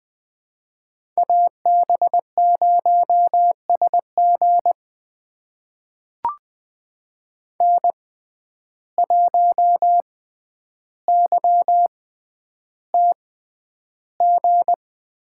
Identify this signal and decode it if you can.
Morse code, heard as a keyed tone, about 20 wpm, spaced as for 10 wpm, 700 Hz. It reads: AB0SG N1YTG
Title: USA Call Signs - 5 Characters - 6x Character Spacing - 20wpm